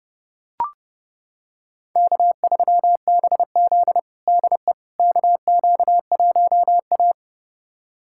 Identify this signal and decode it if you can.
Morse code, heard as a keyed tone, about 32 wpm, 700 Hz. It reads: K3BZ DE KQ1A
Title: USA Callsign DE Callsign - 4 Characters - 30wpm